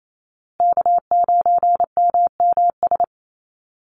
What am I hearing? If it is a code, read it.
Morse code, heard as a keyed tone, about 28 wpm, 700 Hz. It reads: K9MMS
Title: Call Signs Repeated 2x - 28wpm